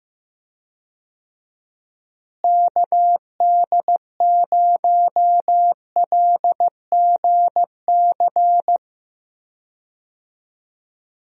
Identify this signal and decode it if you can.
Morse code, heard as a keyed tone, about 15 wpm, 700 Hz. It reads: KD0LGC
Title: USA Call Signs - 6 Characters - Repeated 2x - 15wpm